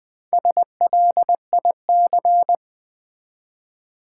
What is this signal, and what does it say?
Morse code, heard as a keyed tone, about 20 wpm, 700 Hz. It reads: SLIC
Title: Build A Word - 1 Letter At A Time - 5-Letter Words - 20wpm